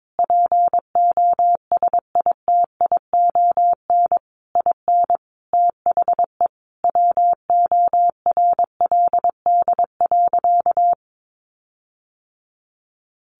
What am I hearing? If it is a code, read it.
Morse code, heard as a keyed tone, about 22 wpm, 700 Hz. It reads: POSITION IN THE WORLD.
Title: Sentences from Top 1300 Words 22wpm